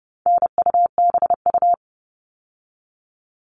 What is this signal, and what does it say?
Morse code, heard as a keyed tone, about 30 wpm, 700 Hz. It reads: NUBU
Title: All Cut Numbers - 4-Digit 30wpm